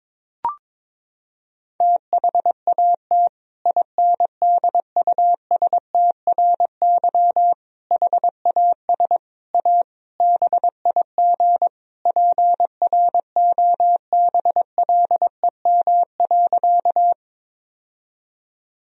Morse code, heard as a keyed tone, about 22 wpm, 700 Hz. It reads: THAT INDUSTRY HAS A BIG PROBLEM.
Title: Sentences from Top 1000 Words - Review Part 4 - 22wpm